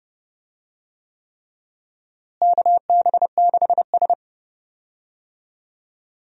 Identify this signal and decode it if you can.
Morse code, heard as a keyed tone, about 29 wpm, 700 Hz. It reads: KB6S
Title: USA Call Signs - 4 Characters - Repeated 3x - 30wpm